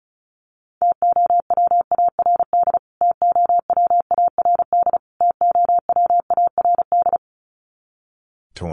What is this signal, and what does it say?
Morse code, heard as a keyed tone, about 34 wpm, 700 Hz. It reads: TOWARD TOWARD TOWARD
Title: Top 500 Words Repeated 35wpm